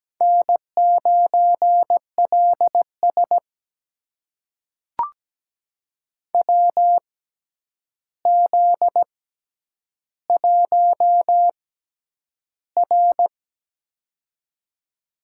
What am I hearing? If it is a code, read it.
Morse code, heard as a keyed tone, about 17 wpm, spaced as for 9 wpm, 700 Hz. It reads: N9LS WZ1R
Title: USA Call Signs - 6x Character Spacing - 17wpm